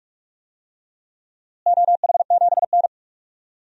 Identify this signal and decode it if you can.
Morse code, heard as a keyed tone, about 46 wpm, 700 Hz. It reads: OH7N
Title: DXCC-100 Call Signs - 4 Characters - Repeated 3x - 45wpm